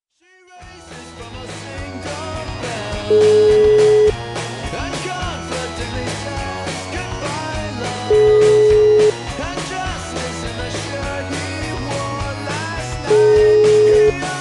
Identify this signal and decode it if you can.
Ringback tone (Europe, 425 Hz), with noise.